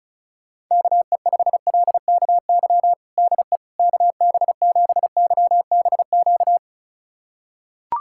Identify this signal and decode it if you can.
Morse code, heard as a keyed tone, about 36 wpm, 700 Hz. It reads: KE5LKY DE KB7YBQ
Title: USA Callsign DE Callsign - 6 Characters - 35wpm